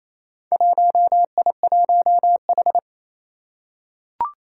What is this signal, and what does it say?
Morse code, heard as a keyed tone, about 27 wpm, 700 Hz. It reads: 1I1H
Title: Four Challenging Letter-Number Combinations 28wpm